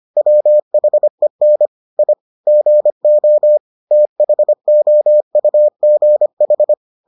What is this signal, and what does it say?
Morse code, heard as a keyed tone, about 25 wpm, 600 Hz. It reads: WHEN I GO THOUGH